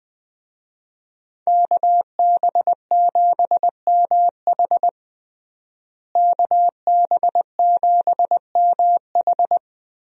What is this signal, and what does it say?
Morse code, heard as a keyed tone, about 20 wpm, 700 Hz. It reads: KB7MH KB7MH
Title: USA Call Signs - 5 Characters - Repeated 3x - 20wpm